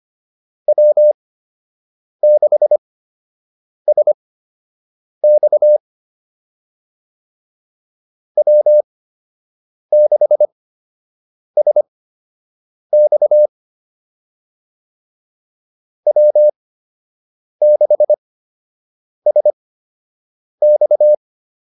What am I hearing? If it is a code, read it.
Morse code, heard as a keyed tone, about 25 wpm, spaced as for 7 wpm, 600 Hz. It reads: W6SX W6SX W6SX